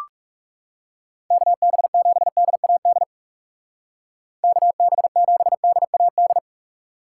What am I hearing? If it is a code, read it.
Morse code, heard as a keyed tone, about 38 wpm, 700 Hz. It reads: KB7DAD KB7DAD
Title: Speed Racing - USA 2x3 Call Signs 30wpm